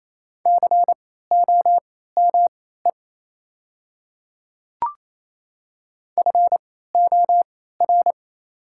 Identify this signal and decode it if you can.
Morse code, heard as a keyed tone, about 28 wpm, spaced as for 16 wpm, 700 Hz. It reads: COME FOR
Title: 3x Character Spacing - Top 100 Words - 28wpm